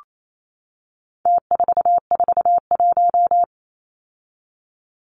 Morse code, heard as a keyed tone, about 28 wpm, 700 Hz. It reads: T441
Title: Numbers - Contest Style - 4-Digit 28wpm